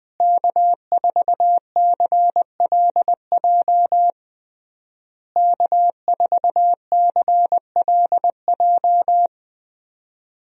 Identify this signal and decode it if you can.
Morse code, heard as a keyed tone, about 20 wpm, 700 Hz. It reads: K4CLJ K4CLJ
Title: USA Call Signs - 5 Characters - Repeated 3x - 20wpm